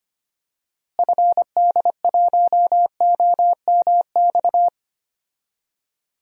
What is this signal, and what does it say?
Morse code, heard as a keyed tone, about 25 wpm, 700 Hz. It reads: FD1OMX
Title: DXCC-100 Call Signs - 6 Characters - Repeated 2x - 25wpm